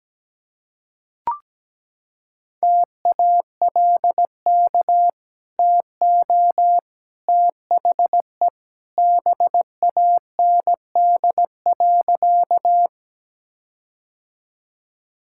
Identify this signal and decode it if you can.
Morse code, heard as a keyed tone, about 17 wpm, 700 Hz. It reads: TALK TO THE BAND.
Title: Sentences from Top 1600 Words 17wpm